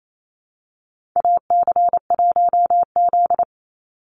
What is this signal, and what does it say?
Morse code, heard as a keyed tone, about 28 wpm, 700 Hz. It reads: AC1Z
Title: NA SOTA Activator Callsigns 28wpm